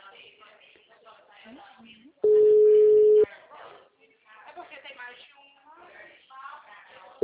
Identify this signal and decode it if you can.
Ringback tone (Europe, 425 Hz), with speech.
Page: Cordeiropolis - SP